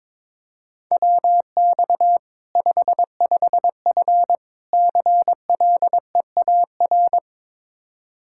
Morse code, heard as a keyed tone, about 22 wpm, 700 Hz. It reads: WX 55F CLEAR
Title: Practice QSO - Exchange 2 - Info - 22wpm